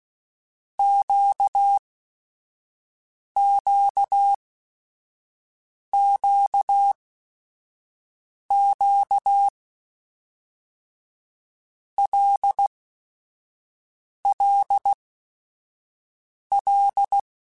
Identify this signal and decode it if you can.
Morse code, heard as a keyed tone, about 16 wpm, 800 Hz. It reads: Q Q Q Q L L L
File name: Basic Sets / Basic Letter Groups Collective7 / LFCYQ@5